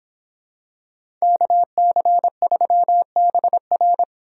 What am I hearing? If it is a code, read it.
Morse code, heard as a keyed tone, about 26 wpm, 700 Hz. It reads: KC3BR